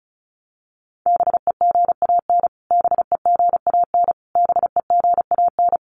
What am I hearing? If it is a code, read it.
Morse code, heard as a keyed tone, about 34 wpm, 700 Hz. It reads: BEGAN BEGAN BEGAN
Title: Top 300 Words Repeated 35wpm